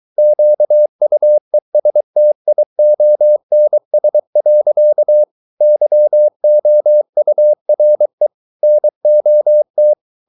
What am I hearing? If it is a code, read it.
Morse code, heard as a keyed tone, about 23 wpm, 600 Hz. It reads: QUESTIONS. YOURE NOT